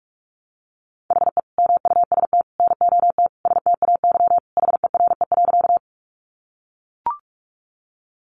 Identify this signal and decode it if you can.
Morse code, heard as a keyed tone, about 44 wpm, 700 Hz. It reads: HE MUST NOT STAY HERE.
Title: Sentences from Top 600 Words 45wpm